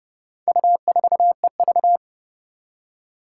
Morse code, heard as a keyed tone, about 32 wpm, 700 Hz. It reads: U4EV
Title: Four Challenging Letter-Number Combinations 30wpm